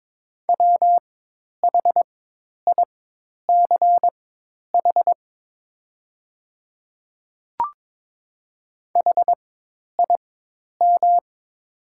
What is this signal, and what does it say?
Morse code, heard as a keyed tone, about 23 wpm, spaced as for 10 wpm, 700 Hz. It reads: WHICH HIM
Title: 4x Character Spacing - Top 100 Words - 22wpm